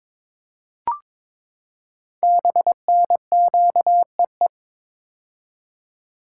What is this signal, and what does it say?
Morse code, heard as a keyed tone, about 22 wpm, 700 Hz. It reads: BNQEE
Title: Code Groups - 5 Letters - 22wpm